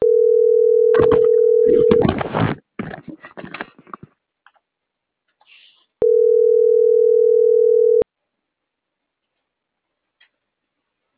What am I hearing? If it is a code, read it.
Ringback tone (North America), 440 + 480 Hz.